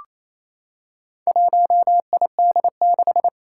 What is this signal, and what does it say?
Morse code, heard as a keyed tone, about 28 wpm, 700 Hz. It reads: 1ID6
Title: Four Challenging Letter-Number Combinations 28wpm